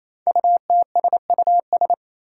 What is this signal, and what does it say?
Morse code, heard as a keyed tone, about 27 wpm, 700 Hz. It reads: UTSUS
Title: Five Challenging Letter-Number Combinations 28wpm